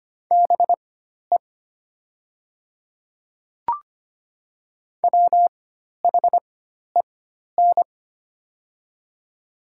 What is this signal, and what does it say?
Morse code, heard as a keyed tone, about 25 wpm, spaced as for 12 wpm, 700 Hz. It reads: BE WHEN
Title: 4x Character Spacing - Top 200 Words - 25wpm